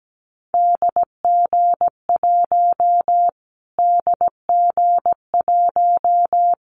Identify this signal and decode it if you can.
Morse code, heard as a keyed tone, about 17 wpm, 700 Hz. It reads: DG1 DG1
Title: Callsign Prefixes for Top 25 DXCC - Repeated 3x - 17wpm